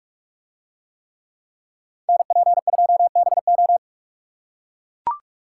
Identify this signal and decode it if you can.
Morse code, heard as a keyed tone, about 40 wpm, 700 Hz. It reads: NP2BO